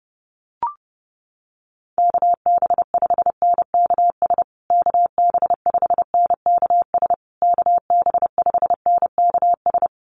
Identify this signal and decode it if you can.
Morse code, heard as a keyed tone, about 30 wpm, 700 Hz. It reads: KB5NKS KB5NKS KB5NKS
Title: USA Call Signs Repeated 3x 30wpm